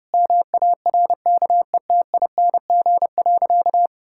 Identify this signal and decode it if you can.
Morse code, heard as a keyed tone, about 30 wpm, 700 Hz. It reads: MARKETING.
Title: Sentences from Top 1000 Words - Review Part 6 - 30wpm